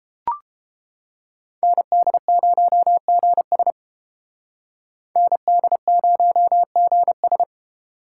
Morse code, heard as a keyed tone, about 32 wpm, 700 Hz. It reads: ND0GS ND0GS